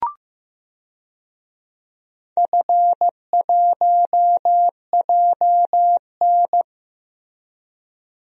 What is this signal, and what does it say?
Morse code, heard as a keyed tone, about 16 wpm, 700 Hz. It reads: F1JN
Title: DXCC-100 Call Signs - 4 Characters - Repeated 2x - 15wpm